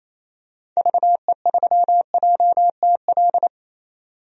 Morse code, heard as a keyed tone, about 28 wpm, 700 Hz. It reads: VE3JTL
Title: DXCC-100 Call Signs - 6 Characters - 28wpm